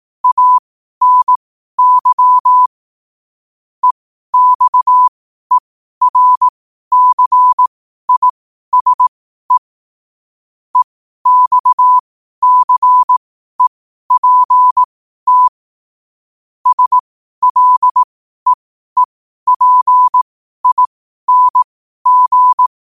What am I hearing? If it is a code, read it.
Morse code, heard as a keyed tone, about 18 wpm, spaced as for 12 wpm, 1000 Hz. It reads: ANY EXERCISE EXCEPT SLEEPING
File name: 12wpm